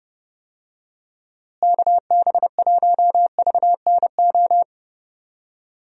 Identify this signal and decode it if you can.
Morse code, heard as a keyed tone, about 30 wpm, 700 Hz. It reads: KB1VNO
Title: USA Call Signs - 6 Characters - Repeated 3x - 30wpm